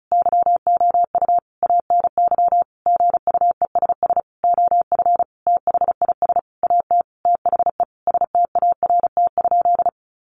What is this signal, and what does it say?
Morse code, heard as a keyed tone, about 34 wpm, 700 Hz. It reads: YOU ANY GUESS OF THIS AT THE START?
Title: Sentences from Top 1500 Words 35wpm